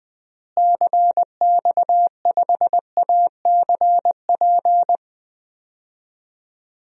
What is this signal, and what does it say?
Morse code, heard as a keyed tone, about 21 wpm, 700 Hz. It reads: CX5ACP
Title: DXCC-100 Call Signs - 6 Characters - Repeated 2x - 20wpm